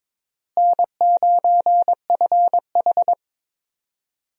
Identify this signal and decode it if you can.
Morse code, heard as a keyed tone, about 22 wpm, 700 Hz. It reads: N9FH
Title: USA Call Signs - 4 Characters - Repeated 3x - 22wpm